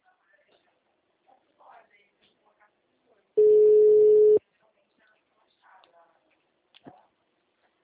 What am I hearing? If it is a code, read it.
Ringback tone (Europe, 425 Hz), with speech.
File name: out- -2004- - - .406